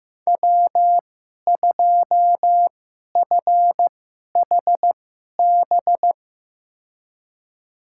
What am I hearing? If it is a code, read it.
Morse code, heard as a keyed tone, about 15 wpm, spaced as for 11 wpm, 700 Hz. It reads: W2FHB
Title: USA Call Signs - 2x Character Spacing - 15wpm